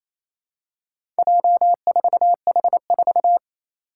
Morse code, heard as a keyed tone, about 28 wpm, 700 Hz. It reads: J4H4
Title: Four Challenging Letter-Number Combinations 28wpm